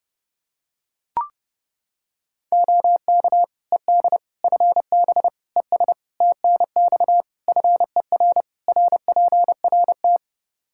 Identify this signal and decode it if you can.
Morse code, heard as a keyed tone, about 30 wpm, 700 Hz. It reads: OK ED FB ES TNX FER RPRT
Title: K1USN SST QSO Practice 30wpm